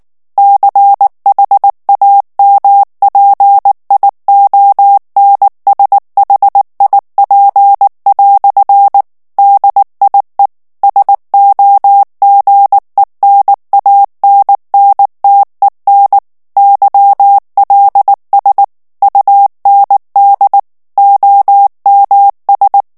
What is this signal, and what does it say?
Morse code, heard as a keyed tone, about 19 wpm, 800 Hz. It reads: CHAMPIONSHIP" DIE SOGENANNTEN YLS UND OMS
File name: morse mp3 / ALLEMAND texte clair 2 vitesse 18 mots minute